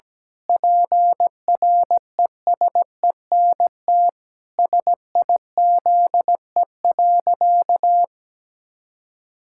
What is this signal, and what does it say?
Morse code, heard as a keyed tone, about 17 wpm, 700 Hz. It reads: PRESENT SIZE.
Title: Sentences from Top 1000 Words - Review Part 6 - 17wpm